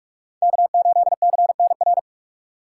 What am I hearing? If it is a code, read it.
Morse code, heard as a keyed tone, about 66 wpm, spaced as for 55 wpm, 700 Hz. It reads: K8CNR
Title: Speed Racing - USA 1x3 Call Signs 45wpm